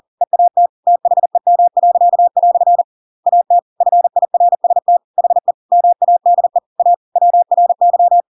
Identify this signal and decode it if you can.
Morse code, heard as a keyed tone, about 40 wpm, 700 Hz. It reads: EAT THEM." AT FIRST HE MADE A WRY
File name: Pinocchio / Pinocchio ch7 40 40 ik1hge